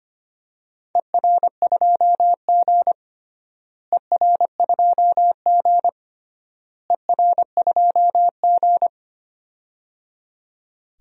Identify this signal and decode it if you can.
Morse code, heard as a keyed tone, about 25 wpm, 700 Hz. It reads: ER2G ER2G ER2G
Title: DXCC-100 Call Signs - 4 Characters - Repeated 3x - 25wpm